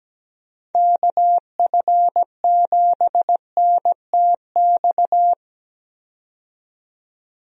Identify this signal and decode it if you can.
Morse code, heard as a keyed tone, about 17 wpm, 700 Hz. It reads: KF7NTX
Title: USA Call Signs - 6 Characters - 17wpm